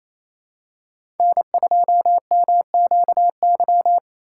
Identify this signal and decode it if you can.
Morse code, heard as a keyed tone, about 28 wpm, 700 Hz. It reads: N2MQY
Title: USA Call Signs - 5 Characters - 2x Character Spacing - 28wpm